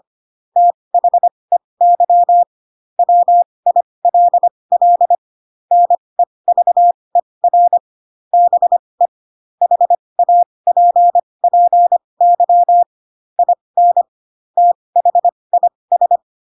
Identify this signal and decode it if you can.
Morse code, heard as a keyed tone, about 25 wpm, spaced as for 20 wpm, 700 Hz. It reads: THEY WILL NEVER BE HAPPY IN THIS
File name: Pinocchio / Pinocchio ch4 15 25 ik1hge